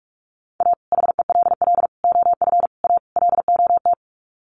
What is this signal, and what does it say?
Morse code, heard as a keyed tone, about 44 wpm, 700 Hz. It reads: A HELL OF A LOT
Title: Sets of 5 Words Encore 45wpm